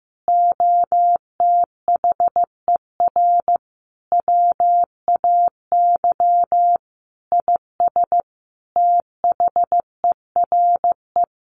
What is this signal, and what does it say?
Morse code, heard as a keyed tone, about 15 wpm, 700 Hz. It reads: OTHER WAY IS THERE
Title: Sentences from Top 100 Words 15wpm